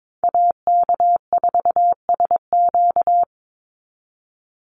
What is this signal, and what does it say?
Morse code, heard as a keyed tone, about 22 wpm, 700 Hz. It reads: AK4SQ
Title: Call Signs Repeated 3x - 22wpm